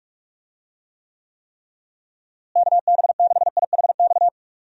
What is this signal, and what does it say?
Morse code, heard as a keyed tone, about 41 wpm, 700 Hz. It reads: KB6IHX
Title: USA Call Signs - 6 Characters - Repeated 3x - 45wpm